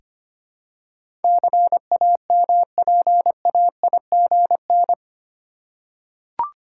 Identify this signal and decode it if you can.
Morse code, heard as a keyed tone, about 25 wpm, 700 Hz. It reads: CAMPAIGN
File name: Eight Letter Words - 25wpm